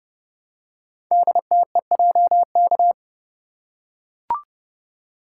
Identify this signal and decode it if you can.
Morse code, heard as a keyed tone, about 30 wpm, 700 Hz. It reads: DTEJK
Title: Code Groups - 5 Letters - 30wpm